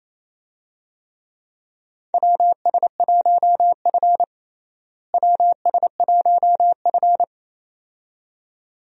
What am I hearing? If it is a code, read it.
Morse code, heard as a keyed tone, about 28 wpm, 700 Hz. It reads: WS1F WS1F
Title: USA Call Signs - 4 Characters - Repeated 2x - 28wpm